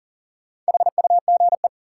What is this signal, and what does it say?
Morse code, heard as a keyed tone, about 40 wpm, 700 Hz. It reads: HUGE
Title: Sentences from Top 1100 Words - New Word Drill - 40wpm